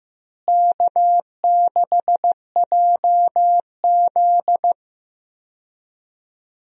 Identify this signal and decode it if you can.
Morse code, heard as a keyed tone, about 15 wpm, 700 Hz. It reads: K6JZ
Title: USA Call Signs - 4 Characters - Repeated 2x - 15wpm